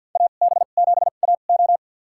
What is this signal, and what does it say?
Morse code, heard as a keyed tone, about 48 wpm, spaced as for 35 wpm, 700 Hz. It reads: AB7AO
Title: USA Call Signs - 5 Characters - 2x Character Spacing - 50wpm